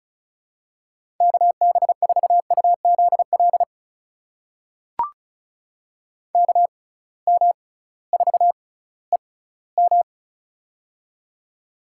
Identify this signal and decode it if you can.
Morse code, heard as a keyed tone, about 36 wpm, spaced as for 19 wpm, 700 Hz. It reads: KB4UZL KM4EM
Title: USA Call Signs - 6x Character Spacing - 35wpm